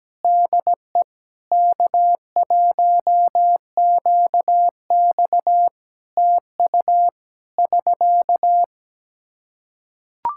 Morse code, heard as a keyed tone, about 17 wpm, 700 Hz. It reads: DE K1QX TU <SK>
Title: Practice QSO - Exchange 4 - Ending - 17wpm